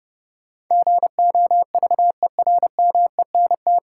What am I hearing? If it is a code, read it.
Morse code, heard as a keyed tone, about 30 wpm, 700 Hz. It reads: GOVERMENT
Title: Sentences from Top 1300 Words - New Word Drill - 30wpm